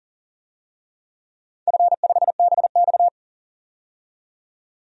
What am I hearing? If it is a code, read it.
Morse code, heard as a keyed tone, about 39 wpm, 700 Hz. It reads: F5BX
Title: DXCC-100 Call Signs - 4 Characters - 40wpm